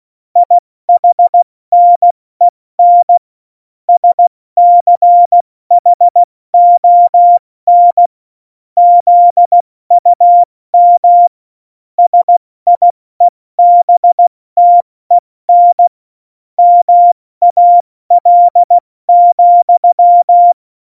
Morse code, heard as a keyed tone, about 16 wpm, 700 Hz. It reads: IHNEN SCHON ZUM SIEBTEN MAL,
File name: Morsekurs / Witze / Witz 25. Februar 2020, 70 F80